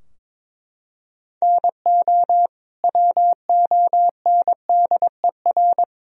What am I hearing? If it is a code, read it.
Morse code, heard as a keyed tone, about 22 wpm, 700 Hz. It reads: NO WONDER
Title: Sentences from Top 1000 Words - Review Part 3 - 22wpm